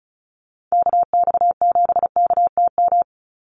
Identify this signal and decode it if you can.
Morse code, heard as a keyed tone, about 34 wpm, 700 Hz. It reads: KX7KTM
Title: USA Call Signs 35wpm